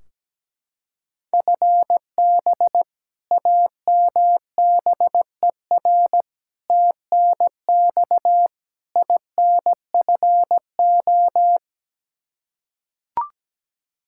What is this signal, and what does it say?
Morse code, heard as a keyed tone, about 17 wpm, 700 Hz. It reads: FB AMBER TNX INFO
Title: Practice QSO - Exchange 3 - More Info - 17wpm